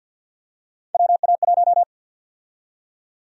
Morse code, heard as a keyed tone, about 51 wpm, 700 Hz. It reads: WA1
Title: USA Call Sign Prefixes - 50wpm